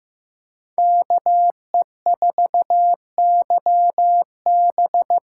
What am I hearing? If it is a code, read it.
Morse code, heard as a keyed tone, about 15 wpm, 700 Hz. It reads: KE4YB
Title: USA Call Signs - 6 Characters - 6x Character Spacing - 15wpm